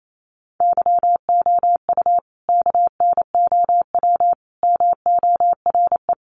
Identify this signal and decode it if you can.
Morse code, heard as a keyed tone, about 28 wpm, 700 Hz. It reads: YOU KNOW MORE
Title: Sentences from Top 100 Words 28wpm